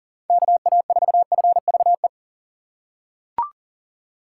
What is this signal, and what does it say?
Morse code, heard as a keyed tone, about 40 wpm, 700 Hz. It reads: KA4FVE
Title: USA Call Signs - 6 Characters - 6x Character Spacing - 40wpm